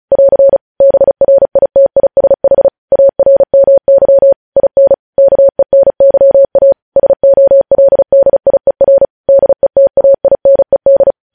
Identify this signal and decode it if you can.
Morse code, heard as a keyed tone, about 36 wpm, 550 Hz. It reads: <AR> BRITISH ARMY IN KENYA SOLDIER DETAINED
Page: news headlines in morse code at 35 wpm